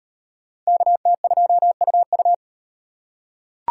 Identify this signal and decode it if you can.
Morse code, heard as a keyed tone, about 36 wpm, 700 Hz. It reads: KT2UU